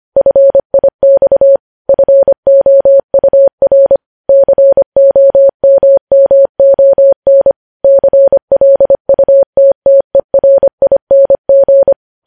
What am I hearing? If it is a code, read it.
Morse code, heard as a keyed tone, about 25 wpm, 550 Hz. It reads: FIX FOUR COMMON CLUTTERING
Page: news-headlines-in-morse-code-at-25-wpm